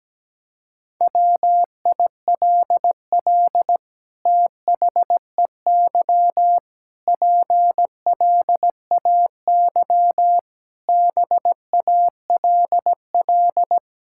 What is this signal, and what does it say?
Morse code, heard as a keyed tone, about 17 wpm, 700 Hz. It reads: WILL THEY PLAY BALL
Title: Sentences from Top 1000 Words - Review Part 6 - 17wpm